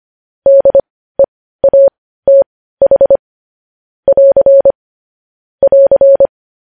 Morse code, heard as a keyed tone, about 25 wpm, spaced as for 15 wpm, 550 Hz. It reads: DEATH <AR> <AR>
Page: news-headlines-in-morse-code-at-15-wpm